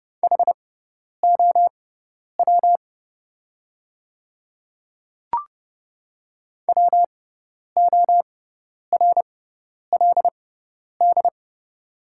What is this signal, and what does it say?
Morse code, heard as a keyed tone, about 30 wpm, spaced as for 10 wpm, 700 Hz. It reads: HOW WORLD
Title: 6x Character Spacing - Top 200 Words - 30wpm